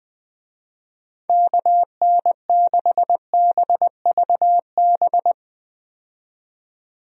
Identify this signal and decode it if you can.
Morse code, heard as a keyed tone, about 20 wpm, 700 Hz. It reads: KN6BVB
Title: USA Call Signs - 6 Characters - 20wpm